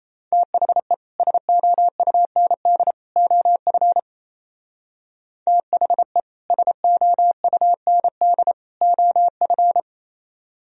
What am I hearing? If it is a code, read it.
Morse code, heard as a keyed tone, about 33 wpm, 700 Hz. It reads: THE SOUND OF THE SOUND OF
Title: Speed-Racing - Sets of 3 Words 28wpm